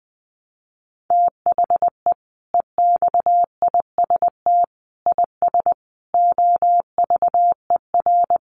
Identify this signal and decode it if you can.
Morse code, heard as a keyed tone, about 20 wpm, 700 Hz. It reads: THE EXIST IS OVER
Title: Sentences from Top 1100 Words 20wpm